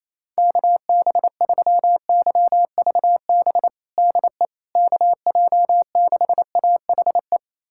Morse code, heard as a keyed tone, about 28 wpm, 700 Hz. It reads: KB3YVB DE KJ6AHE
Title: USA Callsign DE Callsign - 6 Characters - 28wpm